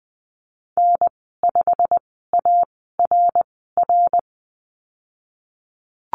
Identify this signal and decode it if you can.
Morse code, heard as a keyed tone, about 20 wpm, spaced as for 14 wpm, 700 Hz. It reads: N5ARR
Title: 088 Lesson 15g - Number 5 - Call Signs 20wpm